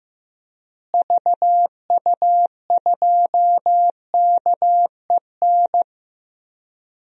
Morse code, heard as a keyed tone, about 16 wpm, 700 Hz. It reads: VU2KEN
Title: DXCC-100 Call Signs - 6 Characters - Repeated 3x - 15wpm